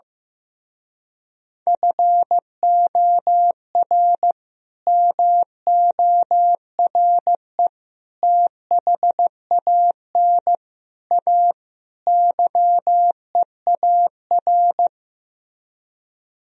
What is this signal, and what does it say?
Morse code, heard as a keyed tone, about 15 wpm, 700 Hz. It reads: FOR MORE THAN A YEAR
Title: Speed-Racing - Sets of 5 Words 15wpm